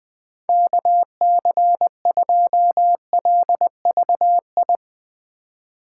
Morse code, heard as a keyed tone, about 21 wpm, 700 Hz. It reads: KC2LVI
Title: USA Call Signs - 6 Characters - 2x Character Spacing - 20wpm